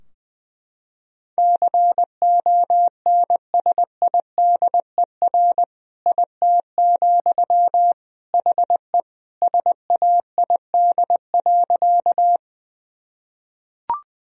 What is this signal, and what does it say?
Morse code, heard as a keyed tone, about 21 wpm, 700 Hz. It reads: CONSIDER IT, HE SAID.
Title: Sentences from Top 1000 Words - Review Part 4 - 20wpm